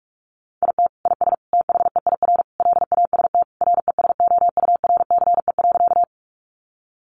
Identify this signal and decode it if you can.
Morse code, heard as a keyed tone, about 44 wpm, 700 Hz. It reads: IT IS THEIR LAST RESOURCE.
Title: Sentences from Top 1300 Words 45wpm